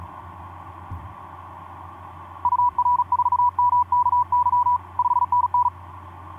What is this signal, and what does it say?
Morse code, heard as a keyed tone, about 40 wpm, 980 Hz. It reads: UR4MKY 5NN
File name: d44ac ek2rgt etc 10m cw 23nov2010 1202